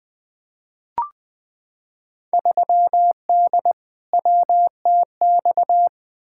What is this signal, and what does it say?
Morse code, heard as a keyed tone, about 20 wpm, 700 Hz. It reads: 3D WTX
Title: Field Day Exchanges Encore - 20wpm